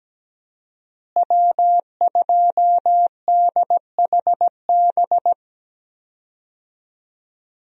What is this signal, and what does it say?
Morse code, heard as a keyed tone, about 17 wpm, 700 Hz. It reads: W2DHB
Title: USA Call Signs - 5 Characters - Repeated 3x - 17wpm